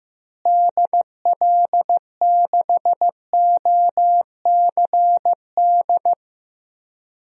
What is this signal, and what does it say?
Morse code, heard as a keyed tone, about 15 wpm, 700 Hz. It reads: DL6OCD
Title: DXCC-100 Call Signs - 6 Characters - Repeated 3x - 15wpm